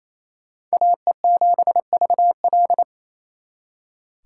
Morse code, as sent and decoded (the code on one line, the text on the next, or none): .- . --... ...- .-..
AE7VL